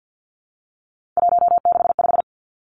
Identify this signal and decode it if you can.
Morse code, heard as a keyed tone, about 50 wpm, 700 Hz. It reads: 165
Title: Three Challenging Letter-Number Combinations 50wpm